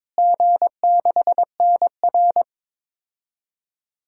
Morse code, heard as a keyed tone, about 22 wpm, 700 Hz. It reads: G6NR
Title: DXCC-100 Call Signs - 4 Characters - Repeated 3x - 22wpm